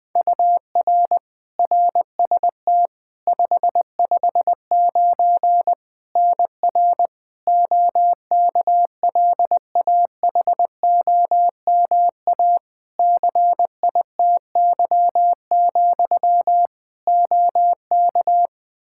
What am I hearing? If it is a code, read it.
Morse code, heard as a keyed tone, about 20 wpm, 700 Hz. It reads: UR RST 559 NR OKLAHOMA CITY, OK
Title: Practice QSO - Exchange 1 - Beginning - 20wpm